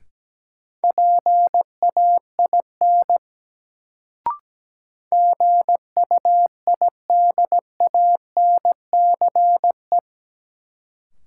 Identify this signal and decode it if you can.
Morse code, heard as a keyed tone, about 17 wpm, 700 Hz. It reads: PAIN GUIDANCE
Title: Sentences from Top 1600 Words - New Word Drill - 17wpm